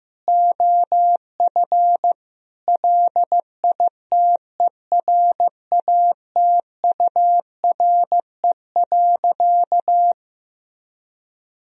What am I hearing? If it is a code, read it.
Morse code, heard as a keyed tone, about 15 wpm, 700 Hz. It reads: OF LITERATURE.
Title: Sentences from Top 1000 Words - Review Part 4 - 15wpm